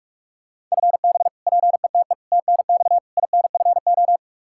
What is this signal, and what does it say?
Morse code, heard as a keyed tone, about 46 wpm, 700 Hz. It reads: FB PETE TNX INFO
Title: Practice QSO - Exchange 3 - More Info - 45wpm